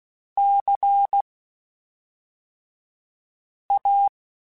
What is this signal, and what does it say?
Morse code, heard as a keyed tone, about 16 wpm, spaced as for 3 wpm, 800 Hz. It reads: CA
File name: (spaced) / QSOs Spaced-mp3s / WA7QOX 1 @5 sp